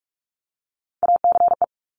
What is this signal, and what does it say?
Morse code, heard as a keyed tone, about 46 wpm, 700 Hz. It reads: ACE
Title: Top Three Letter Combinations in Words 45wpm